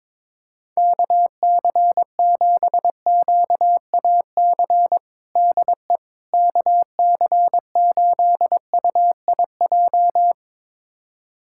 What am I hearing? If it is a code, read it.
Morse code, heard as a keyed tone, about 22 wpm, 700 Hz. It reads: KC7QAC DE KC8UIJ
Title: USA Callsign DE Callsign - 6 Characters - 22wpm